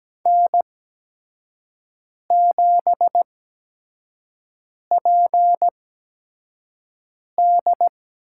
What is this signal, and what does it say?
Morse code, heard as a keyed tone, about 17 wpm, spaced as for 5 wpm, 700 Hz. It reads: N7PD
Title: USA Call Signs - 5 Characters - 8x Character Spacing - 17wpm